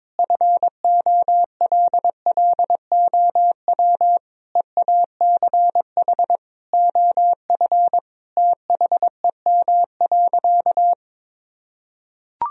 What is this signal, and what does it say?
Morse code, heard as a keyed tone, about 22 wpm, 700 Hz. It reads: FOLLOW EACH OF THEM.
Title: Sentences from Top 1000 Words - Review Part 1 - 22wpm